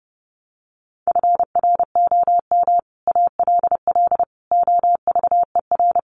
Morse code, heard as a keyed tone, about 30 wpm, 700 Hz. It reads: FROM ALL OVER
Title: Sets of 5 Words Encore 30wpm